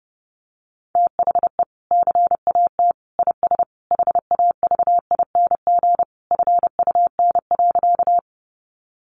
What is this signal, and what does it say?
Morse code, heard as a keyed tone, about 28 wpm, 700 Hz. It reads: THE CAT IS HAVING FUN.
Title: Easy Phrases 30wpm